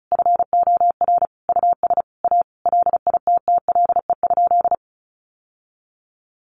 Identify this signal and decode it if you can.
Morse code, heard as a keyed tone, about 34 wpm, 700 Hz. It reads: FOR US A LITTLE?
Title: Sentences from Top 900 Words 35wpm